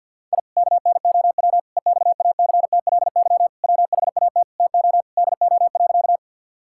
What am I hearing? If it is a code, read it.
Morse code, heard as a keyed tone, about 50 wpm, 700 Hz. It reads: I KNOW EXACTLY WHAT TO DO.
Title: Sentences from Top 1000 Words - Review Part 6 - 50wpm